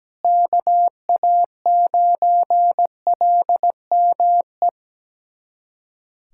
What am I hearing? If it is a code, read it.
Morse code, heard as a keyed tone, about 17 wpm, 700 Hz. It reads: KA9LME
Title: USA Callsign DE Callsign 17wpm